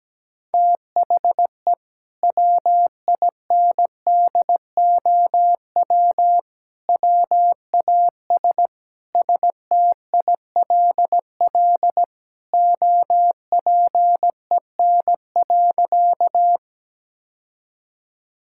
Morse code, heard as a keyed tone, about 17 wpm, 700 Hz. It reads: THE WINDOW WAS STILL OPEN.
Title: Sentences from Top 1600 Words 17wpm